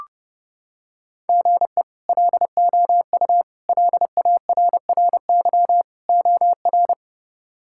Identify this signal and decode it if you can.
Morse code, heard as a keyed tone, about 30 wpm, 700 Hz. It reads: GE LOU LARRY OR
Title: K1USN SST QSO Practice - 30wpm